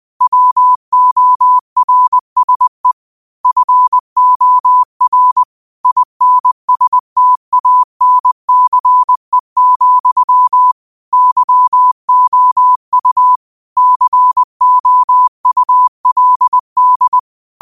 Morse code, heard as a keyed tone, about 20 wpm, 1000 Hz. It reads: WORSE FOR INSTANCE, YOU COULD
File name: qotdfiles / 20wpm